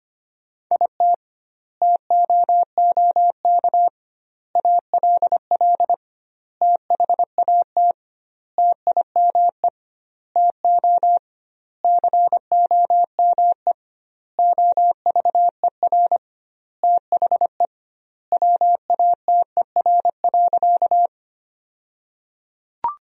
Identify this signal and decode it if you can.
Morse code, heard as a keyed tone, about 25 wpm, 700 Hz. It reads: IT TOOK ALL THAT TIME TO COME OVER THE WATER.
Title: 2x Word-Spacing - Sentences from Top 300 Words - 25wpm